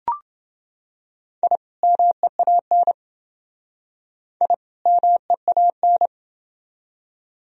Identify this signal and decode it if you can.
Morse code, heard as a keyed tone, about 29 wpm, 700 Hz. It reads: I MEAN I MEAN